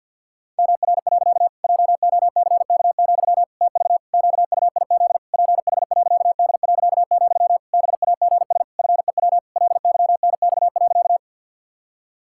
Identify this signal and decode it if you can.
Morse code, heard as a keyed tone, about 51 wpm, 700 Hz. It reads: MR. JOCK, TV QUIZ PH.D., BAGS FEW LYNX.
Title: Pangram Challenge v2 - 50wpm